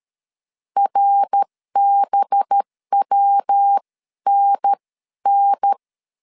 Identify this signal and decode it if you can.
Morse code, heard as a keyed tone, about 13 wpm, spaced as for 11 wpm, 800 Hz. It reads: RBWNN
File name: CW / DET / 1 6 1500C S C 12wpm 3533